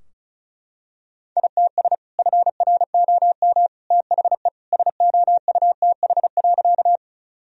Morse code, heard as a keyed tone, about 36 wpm, 700 Hz. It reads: ITS FROM THE SOUTH.
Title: Sentences from Top 1000 Words - Review Part 2 - 35wpm